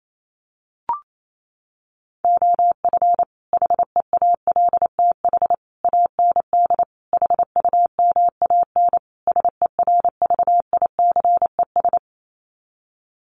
Morse code, heard as a keyed tone, about 28 wpm, 700 Hz. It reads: OF HEALTH AND HUMAN SERVICES
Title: Sets of 5 Words 28wpm